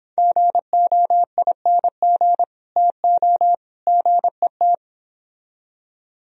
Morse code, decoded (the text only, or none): GOING TO GET